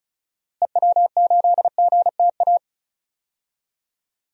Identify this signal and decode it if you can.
Morse code, heard as a keyed tone, about 36 wpm, 700 Hz. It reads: EW8GTA
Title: DXCC-100 Call Signs - 6 Characters - Repeated 3x - 35wpm